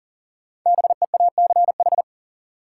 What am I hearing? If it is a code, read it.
Morse code, heard as a keyed tone, about 40 wpm, 700 Hz. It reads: BEACH
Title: Sentences from Top 1500 Words - New Word Drill Alt - 40wpm